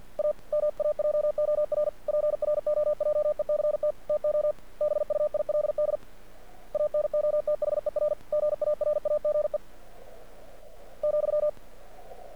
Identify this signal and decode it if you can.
Morse code, heard as a keyed tone, about 47 wpm, 600 Hz. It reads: A MAJOR PROJECT TO BUILD ANOTHER GARAGE ,
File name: archived / CRSnet